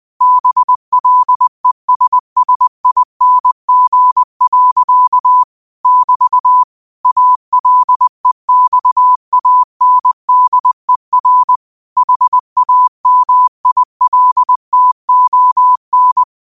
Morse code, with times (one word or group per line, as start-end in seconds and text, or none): 0.2-5.4 BLESSING.
5.9-6.6 <BT>
7.1-11.6 ALEXANDER
12.0-16.2 HAMILTON